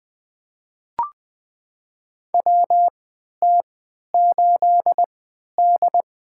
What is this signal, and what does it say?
Morse code, heard as a keyed tone, about 20 wpm, spaced as for 11 wpm, 700 Hz. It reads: WT8D
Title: USA Call Signs - 4 Characters - 3x Character Spacing - 20wpm